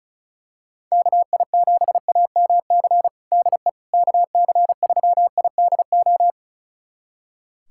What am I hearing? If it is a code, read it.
Morse code, heard as a keyed tone, about 34 wpm, 700 Hz. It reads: KI7AMC DE KC3IDO
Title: USA Callsign DE Callsign 35wpm